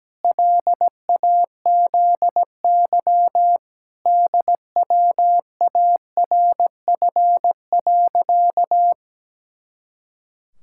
Morse code, heard as a keyed tone, about 17 wpm, 700 Hz. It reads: LAZY DWARF.
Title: Pangram Challenge v2 - 17wpm